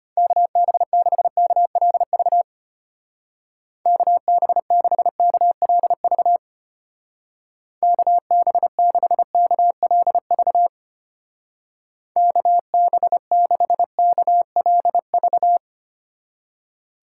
Morse code, heard as a keyed tone, about 31 wpm, 700 Hz. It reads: KB6KLV KB6KLV KB6KLV KB6KLV
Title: Speed Racing - USA Call Signs 25wpm